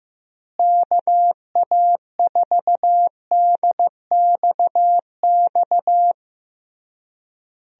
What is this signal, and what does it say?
Morse code, heard as a keyed tone, about 15 wpm, 700 Hz. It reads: KA4DXX
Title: USA Call Signs - 6 Characters - Repeated 3x - 15wpm